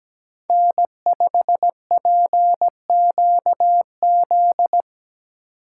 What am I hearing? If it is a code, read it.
Morse code, heard as a keyed tone, about 20 wpm, 700 Hz. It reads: N5PQZ
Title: Speed Racing - USA 1x3 Call Signs 17wpm